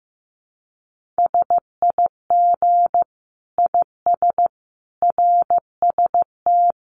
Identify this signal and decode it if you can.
Morse code, heard as a keyed tone, about 15 wpm, 700 Hz. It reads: SIG IS RST
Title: Common QSO Phrases 15wpm